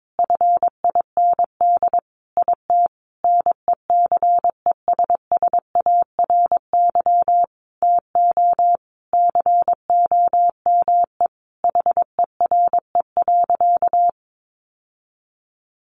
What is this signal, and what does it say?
Morse code, heard as a keyed tone, about 22 wpm, 700 Hz. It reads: FIND IT NECESSARY TO COME HERE.
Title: Sentences from Top 1300 Words 22wpm